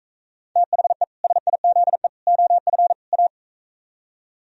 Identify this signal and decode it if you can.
Morse code, heard as a keyed tone, about 45 wpm, 700 Hz. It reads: THE SIZE OF A